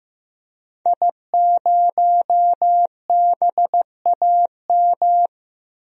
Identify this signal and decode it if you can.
Morse code, heard as a keyed tone, about 15 wpm, 700 Hz. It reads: I0BAM
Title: DXCC-100 Call Signs - 5 Characters - Repeated 3x - 15wpm